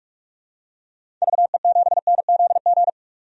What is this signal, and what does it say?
Morse code, heard as a keyed tone, about 41 wpm, 700 Hz. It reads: VE7NZG
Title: DXCC-100 Call Signs - 6 Characters - 45wpm